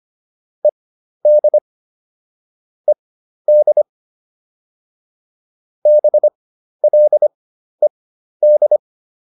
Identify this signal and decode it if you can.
Morse code, heard as a keyed tone, about 25 wpm, spaced as for 12 wpm, 600 Hz. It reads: ED ED BLED